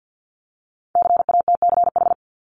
Morse code, heard as a keyed tone, about 50 wpm, 700 Hz. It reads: CATCH
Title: Multi-letter Phonemes 50wpm